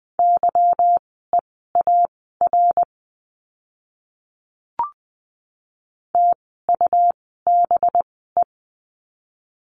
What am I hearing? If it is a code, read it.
Morse code, heard as a keyed tone, about 20 wpm, spaced as for 15 wpm, 700 Hz. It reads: YEAR TUBE
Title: 185 Lesson 30f - Letter b - Words 20wpm